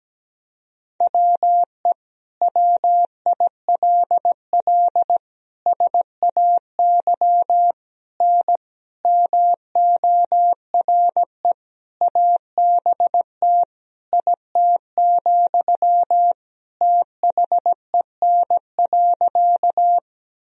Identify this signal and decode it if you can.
Morse code, heard as a keyed tone, about 17 wpm, 700 Hz. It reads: WE WILL SAY N MORE ABT IT, THEN.
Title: Sentences from Top 200 Words and Abbreviations 17wpm